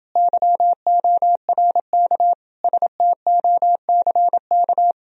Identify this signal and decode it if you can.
Morse code, heard as a keyed tone, about 30 wpm, 700 Hz. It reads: YORK STOCK